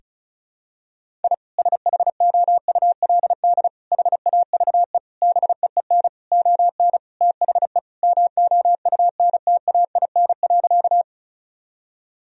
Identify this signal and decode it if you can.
Morse code, heard as a keyed tone, about 34 wpm, 700 Hz. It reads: I SHOULD HAVE BEEN ON THE MOUNTAIN.
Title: Sentences from Top 1000 Words - Review Part 1 - 35wpm